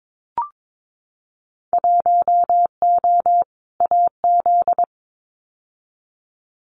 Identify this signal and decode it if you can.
Morse code, heard as a keyed tone, about 22 wpm, 700 Hz. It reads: 1O AZ
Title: Winter Field Day Exchanges 22wpm